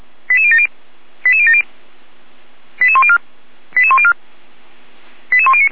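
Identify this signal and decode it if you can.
Five-tone selective call (ZVEI), heard as quick runs of five tones, tones of about 70 ms. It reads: 88070 88070 88104 88104 88109
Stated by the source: ZVEI five-tone selective call received by SDR.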